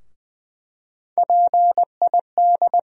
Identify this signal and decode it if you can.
Morse code, heard as a keyed tone, about 20 wpm, 700 Hz. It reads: PID
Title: USA Call Sign Postfixes - 3 Letters 20wpm